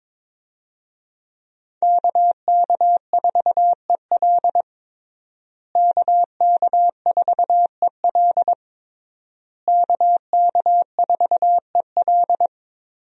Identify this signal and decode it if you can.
Morse code, heard as a keyed tone, about 22 wpm, 690 Hz. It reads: KK4EL KK4EL KK4EL
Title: USA Call Signs - 5 Characters - Repeated 3x - 22wpm